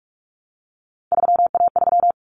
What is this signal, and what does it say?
Morse code, heard as a keyed tone, about 44 wpm, 700 Hz. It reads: 3A3
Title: Common Cut Numbers - 3-Digit 45wpm